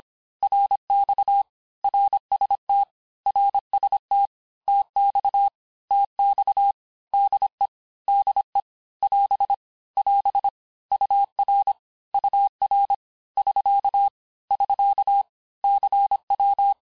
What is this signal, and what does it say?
Morse code, heard as a keyed tone, about 26 wpm, 800 Hz. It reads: RX RST RST TX TX DE DE <AS> <AS> UR UR <SK> <SK> CW
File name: cours-du-lundi files / cours-1584-texte-abreviations-vitesse-19mpm-puis-24mpm